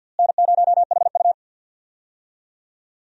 Morse code, heard as a keyed tone, about 52 wpm, 700 Hz. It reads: N0HU
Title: USA Call Signs - 4 Characters - Repeated 2x - 50wpm